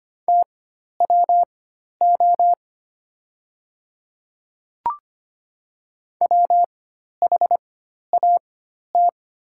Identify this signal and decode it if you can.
Morse code, heard as a keyed tone, about 25 wpm, spaced as for 12 wpm, 700 Hz. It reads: TWO WHAT
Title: 4x Character Spacing - Top 100 Words - 25wpm